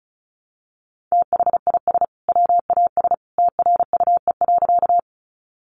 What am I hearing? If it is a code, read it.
Morse code, heard as a keyed tone, about 34 wpm, 700 Hz. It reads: THIS WAS TRUE.
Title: Sentences from Top 400 Words 35wpm